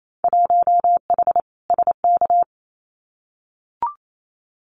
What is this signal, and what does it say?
Morse code, heard as a keyed tone, about 28 wpm, 700 Hz. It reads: 1H SK
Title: Winter Field Day Exchanges 28wpm